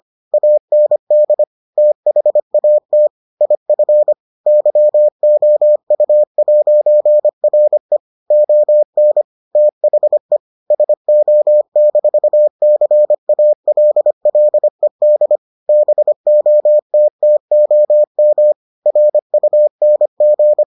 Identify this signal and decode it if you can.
Morse code, heard as a keyed tone, about 25 wpm, 600 Hz. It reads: AND THAT IF YOU'RE ON THE SO-CALLED BOTTOM RUNG